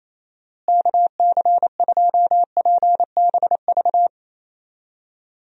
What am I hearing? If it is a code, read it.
Morse code, heard as a keyed tone, about 28 wpm, 700 Hz. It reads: KC2PBV
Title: USA Call Signs - 6 Characters - Repeated 3x - 28wpm